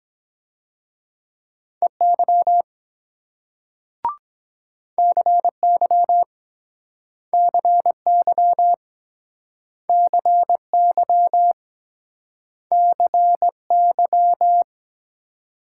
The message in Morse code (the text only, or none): EY CY CY CY CY